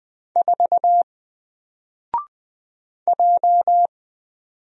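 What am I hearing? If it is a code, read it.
Morse code, heard as a keyed tone, about 20 wpm, 700 Hz. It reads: 4 J
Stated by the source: Single letter and number Morse drill at 20 wpm, from N4LCD.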